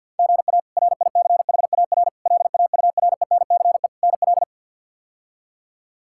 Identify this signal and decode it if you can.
Morse code, heard as a keyed tone, about 53 wpm, 700 Hz. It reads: GA RICHAR LAURENCE NL
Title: K1USN SST QSO Practice - 50wpm